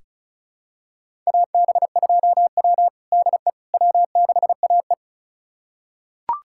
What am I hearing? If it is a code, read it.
Morse code, heard as a keyed tone, about 36 wpm, 700 Hz. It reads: AB2W DE W6AE
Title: USA Callsign DE Callsign - 4 Characters - 35wpm